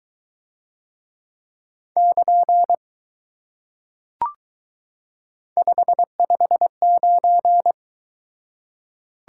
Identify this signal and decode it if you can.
Morse code, heard as a keyed tone, about 22 wpm, 700 Hz. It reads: <KN> 559